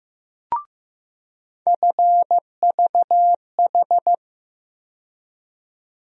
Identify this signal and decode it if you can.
Morse code, heard as a keyed tone, about 15 wpm, 700 Hz. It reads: FVH
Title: USA Call Sign Postfixes - 3 Letters 15wpm